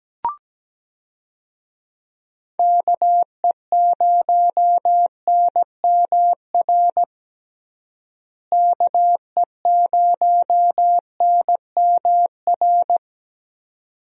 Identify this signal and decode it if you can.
Morse code, heard as a keyed tone, about 17 wpm, 700 Hz. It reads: KE0NMR KE0NMR
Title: USA Call Signs - 6 Characters - Repeated 2x - 17wpm